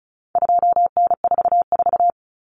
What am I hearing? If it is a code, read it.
Morse code, heard as a keyed tone, about 34 wpm, 700 Hz. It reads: 2N44
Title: Common Cut Numbers - 4-Digit 35wpm